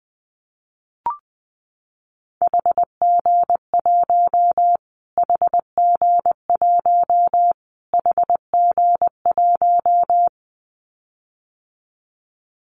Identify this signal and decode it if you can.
Morse code, heard as a keyed tone, about 20 wpm, 700 Hz. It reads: HG1 HG1 HG1
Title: Callsign Prefixes for Top 25 DXCC - Repeated 3x - 20wpm